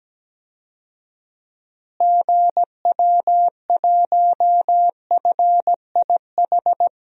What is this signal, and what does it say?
Morse code, heard as a keyed tone, about 17 wpm, 700 Hz. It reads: GW1FIH
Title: DXCC-100 Call Signs - 6 Characters - Repeated 3x - 17wpm